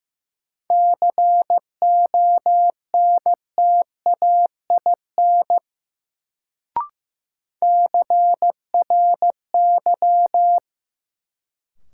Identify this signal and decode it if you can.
Morse code, heard as a keyed tone, about 15 wpm, 700 Hz. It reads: CONTAIN CRY
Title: Sentences from Top 500 Words - New Word Drill - 15wpm